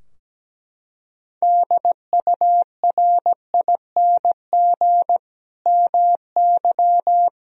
Morse code, heard as a keyed tone, about 17 wpm, 700 Hz. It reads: DURING MY
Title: Sentences from Top 1000 Words - Review Part 2 - 17wpm